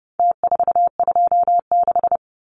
Morse code, heard as a keyed tone, about 30 wpm, 700 Hz. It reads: T426
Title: Numbers - Contest Style - 4-Digit 30wpm